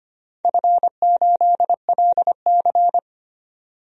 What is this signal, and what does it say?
Morse code, heard as a keyed tone, about 25 wpm, 700 Hz. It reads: F8LC
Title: DXCC-100 Call Signs - 4 Characters - 25wpm